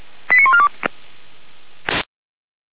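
Five-tone selective call (ZVEI), heard as quick runs of five tones, tones of about 70 ms. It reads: 89142
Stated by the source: ZVEI five-tone selective call received with an SDR.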